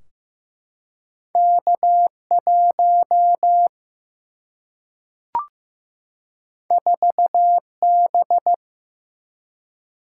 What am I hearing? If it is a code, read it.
Morse code, heard as a keyed tone, about 15 wpm, 700 Hz. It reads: K 1 4 B
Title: Single-Letter Single-Number 15wpm